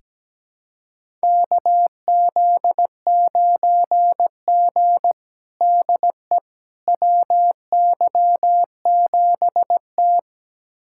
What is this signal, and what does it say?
Morse code, heard as a keyed tone, about 17 wpm, 700 Hz. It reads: KZ9G DE WY7T
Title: USA Callsign DE Callsign - 4 Characters - 17wpm